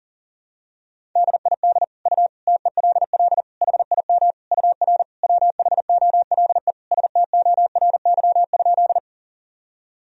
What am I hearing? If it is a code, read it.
Morse code, heard as a keyed tone, about 37 wpm, 700 Hz. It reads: DID U TELL HIM UR WHOLE STORY?
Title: Sentences from Top 400 Words and Abbreviations 40wpm